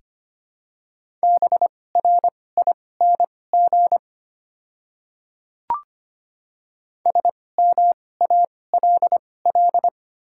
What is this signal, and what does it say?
Morse code, heard as a keyed tone, about 25 wpm, spaced as for 18 wpm, 700 Hz. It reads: BRING SMALL
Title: 2x Character Spacing - Top 500 Words - 25wpm